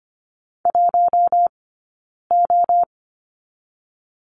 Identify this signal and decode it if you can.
Morse code, heard as a keyed tone, about 25 wpm, 700 Hz. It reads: 1 O
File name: Lesson 17d - Letter u - Two Characters 25wpm